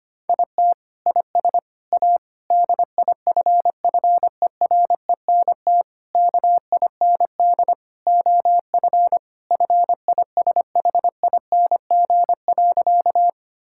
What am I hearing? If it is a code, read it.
Morse code, heard as a keyed tone, about 25 wpm, 700 Hz. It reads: IT IS A DIFFERENT KIND OF FISHING.
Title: Sentences from Top 1000 Words - Review Part 5 - 25wpm